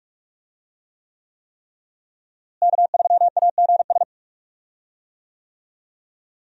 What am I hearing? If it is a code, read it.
Morse code, heard as a keyed tone, about 47 wpm, 700 Hz. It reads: K3AGS
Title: USA Call Signs - 5 Characters - Repeated 3x - 45wpm